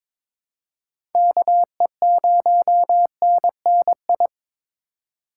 Morse code, heard as a keyed tone, about 21 wpm, 700 Hz. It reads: KE0NNI
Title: USA Call Signs - 6 Characters - 4x Character Spacing - 22wpm